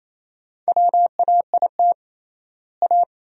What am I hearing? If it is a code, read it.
Morse code, heard as a keyed tone, about 28 wpm, 700 Hz. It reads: WAIT A
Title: 3x Word-Spacing - Sentences from Top 500 Words - 28wpm